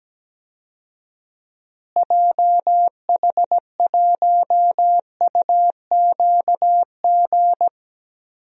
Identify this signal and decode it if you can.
Morse code, heard as a keyed tone, about 17 wpm, 700 Hz. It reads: JH1UQG
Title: DXCC-100 Call Signs - 6 Characters - Repeated 3x - 17wpm